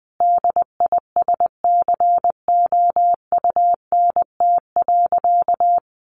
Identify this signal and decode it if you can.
Morse code, heard as a keyed tone, about 20 wpm, 700 Hz. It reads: DISCOUNT.
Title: Sentences from Top 1200 Words 20wpm